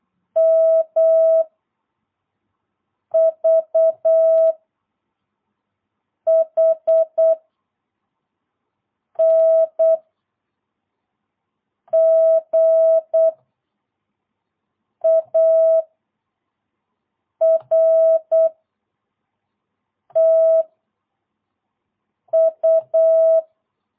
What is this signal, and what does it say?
Morse code, heard as a keyed tone, about 8 wpm, 660 Hz. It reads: M V H N G A R T U
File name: jh8cbh / 2022cw